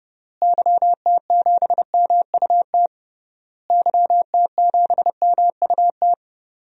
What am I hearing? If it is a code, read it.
Morse code, heard as a keyed tone, about 30 wpm, 700 Hz. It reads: YT7MUT YT7MUT
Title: DXCC-100 Call Signs - 6 Characters - Repeated 3x - 30wpm